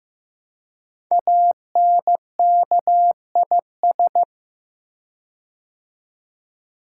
Morse code, heard as a keyed tone, about 15 wpm, 700 Hz. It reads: ANKIS
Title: Code Groups - 5 Letters - 15wpm